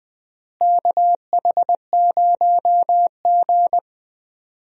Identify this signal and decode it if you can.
Morse code, heard as a keyed tone, about 20 wpm, 700 Hz. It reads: KH0G
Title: USA Call Signs - 4 Characters - Repeated 3x - 20wpm